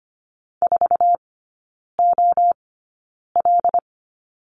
Morse code, heard as a keyed tone, about 25 wpm, spaced as for 9 wpm, 700 Hz. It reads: 4OL
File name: course / 25:10wpm / 093 Lesson 16e - Letter c - Three Characters 25wpm